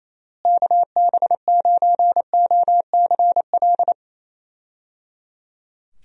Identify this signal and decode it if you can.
Morse code, heard as a keyed tone, about 28 wpm, 700 Hz. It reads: KB9OCL
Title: USA Call Signs - 6 Characters - Repeated 3x - 28wpm